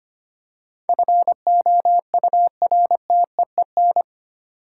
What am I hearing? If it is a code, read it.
Morse code, heard as a keyed tone, about 25 wpm, 700 Hz. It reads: FOURTEEN
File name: 8 Letter Words - First 3 Letters Given - 25wpm